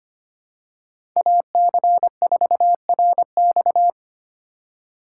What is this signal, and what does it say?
Morse code, heard as a keyed tone, about 25 wpm, 700 Hz. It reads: AC4RX
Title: USA Call Signs - 5 Characters - 2x Character Spacing - 25wpm